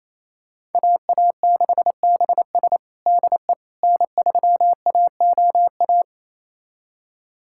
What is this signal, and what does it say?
Morse code, heard as a keyed tone, about 28 wpm, 700 Hz. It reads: AA6BS DE N3AOA
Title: USA Callsign DE Callsign - 5 Characters - 28wpm